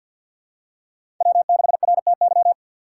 Morse code, heard as a keyed tone, about 46 wpm, 700 Hz. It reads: W6RTY
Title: USA Call Signs - 2x Character Spacing - 50wpm